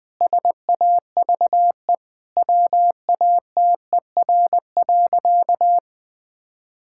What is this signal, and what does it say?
Morse code, heard as a keyed tone, about 20 wpm, 700 Hz. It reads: SAVE WATER.
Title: Difficult Sentences with Hints - 20wpm